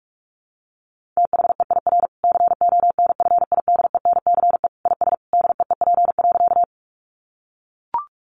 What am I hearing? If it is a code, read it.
Morse code, heard as a keyed tone, about 44 wpm, 700 Hz. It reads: THEIR CONFIDENCE IS DEEP.
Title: Sentences from Top 1500 Words 45wpm